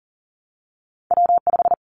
Morse code, heard as a keyed tone, about 38 wpm, 700 Hz. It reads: W5
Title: Callsign Prefixes for Top 25 DXCC - 40wpm